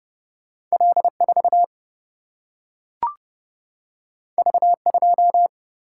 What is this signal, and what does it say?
Morse code, heard as a keyed tone, about 30 wpm, 700 Hz. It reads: L4 V2
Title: Single-Letter Single-Number 30wpm